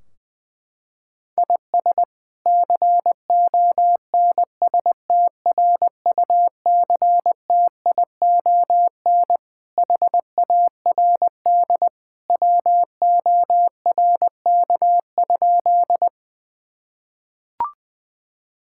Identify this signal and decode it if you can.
Morse code, heard as a keyed tone, about 21 wpm, 700 Hz. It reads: IS CONSTRUCTION HARD WORK?
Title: Sentences from Top 1600 Words 20wpm